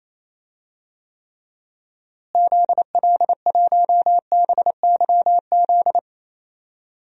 Morse code, heard as a keyed tone, about 27 wpm, 700 Hz. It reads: ZL1BYZ
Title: DXCC-100 Call Signs - 6 Characters - Repeated 2x - 28wpm